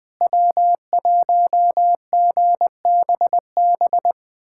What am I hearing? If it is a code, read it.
Morse code, heard as a keyed tone, about 20 wpm, 700 Hz. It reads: W1GBB
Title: USA Callsign DE Callsign 20wpm